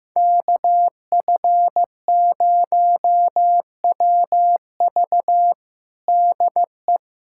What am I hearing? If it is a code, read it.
Morse code, heard as a keyed tone, about 15 wpm, 700 Hz. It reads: KF0WV DE
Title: USA Callsign DE Callsign 15wpm